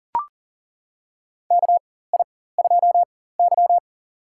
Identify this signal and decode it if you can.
Morse code, heard as a keyed tone, about 37 wpm, spaced as for 18 wpm, 700 Hz. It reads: KI2Y
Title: USA Call Signs - 4 Characters - 4x Character Spacing - 40wpm